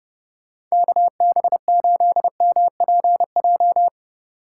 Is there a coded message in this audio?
Yes: Morse code from 0.7 to 3.9 s, heard as a keyed tone, about 39 wpm, 700 Hz. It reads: KB8MPJ